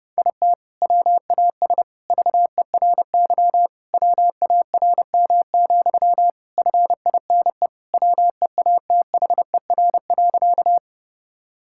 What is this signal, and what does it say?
Morse code, heard as a keyed tone, about 31 wpm, 700 Hz. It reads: IT WAS VERY WARM, FINE WEATHER.
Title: Sentences from Top 1000 Words - Review Part 5 - 30wpm